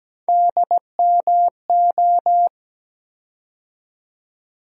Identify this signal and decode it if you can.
Morse code, heard as a keyed tone, about 17 wpm, 700 Hz. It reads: DMO
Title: USA Call Sign Postfixes - 3 Letters 17wpm